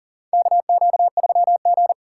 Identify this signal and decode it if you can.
Morse code, heard as a keyed tone, about 40 wpm, 700 Hz. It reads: KQ3G
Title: USA Call Signs - 4 Characters - 4x Character Spacing - 40wpm